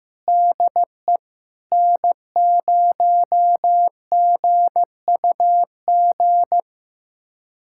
Morse code, heard as a keyed tone, about 15 wpm, 700 Hz. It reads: DE N0GUG
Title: USA Callsign DE Callsign - 5 Characters - 15wpm